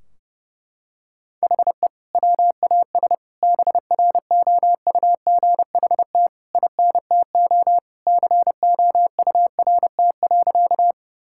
Morse code, heard as a keyed tone, about 30 wpm, 700 Hz. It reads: HE WAS BROUGHT INTO COURT.
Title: Sentences from Top 1600 Words 30wpm